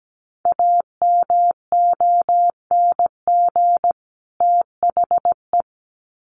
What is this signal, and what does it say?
Morse code, heard as a keyed tone, about 17 wpm, 700 Hz. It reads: AMONG THE
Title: Sets of 2 Words 17wpm